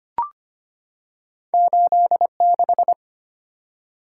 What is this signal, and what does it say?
Morse code, heard as a keyed tone, about 25 wpm, 690 Hz. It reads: 86
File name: Two Letter-Number Combinations - 25wpm